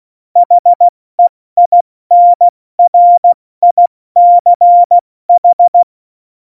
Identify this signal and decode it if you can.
Morse code, heard as a keyed tone, about 16 wpm, 700 Hz. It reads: HEINRICH
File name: Morsekurs / Witze / Witz 27. August 2019, 70 F80